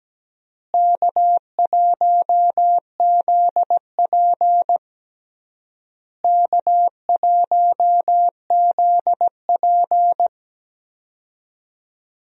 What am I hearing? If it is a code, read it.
Morse code, heard as a keyed tone, about 17 wpm, 700 Hz. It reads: K1ZP K1ZP
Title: USA Call Signs - 4 Characters - Repeated 3x - 17wpm